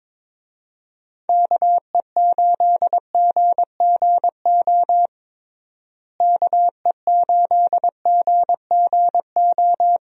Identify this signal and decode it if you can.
Morse code, heard as a keyed tone, about 22 wpm, 700 Hz. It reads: KE8GGO KE8GGO
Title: USA Call Signs - 6 Characters - Repeated 2x - 22wpm